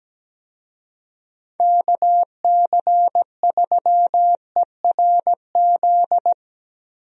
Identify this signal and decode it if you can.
Morse code, heard as a keyed tone, about 17 wpm, 700 Hz. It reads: KC3ERZ
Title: USA Call Signs - 6 Characters - Repeated 3x - 17wpm